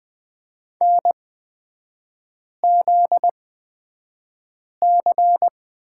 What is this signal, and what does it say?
Morse code, heard as a keyed tone, about 20 wpm, spaced as for 5 wpm, 700 Hz. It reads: NZC
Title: 243 Lesson 39e - Letter z - Three Characters 20wpm